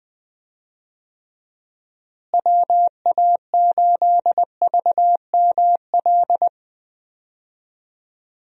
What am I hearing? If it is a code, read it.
Morse code, heard as a keyed tone, about 21 wpm, 700 Hz. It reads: WA8VML
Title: USA Call Signs - 6 Characters - Repeated 3x - 20wpm